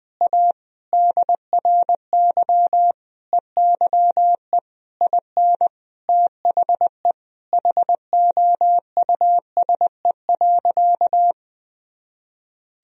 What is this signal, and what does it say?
Morse code, heard as a keyed tone, about 21 wpm, 700 Hz. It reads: A DRY EYE IN THE HOUSE.
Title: Sentences from Top 1000 Words - Review Part 3 - 20wpm